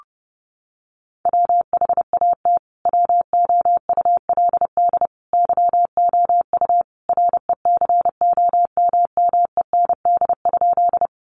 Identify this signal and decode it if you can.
Morse code, heard as a keyed tone, about 30 wpm, 700 Hz. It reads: WHAT WOULD YOU RECOMMEND?
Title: Sentences from Top 1200 Words 30wpm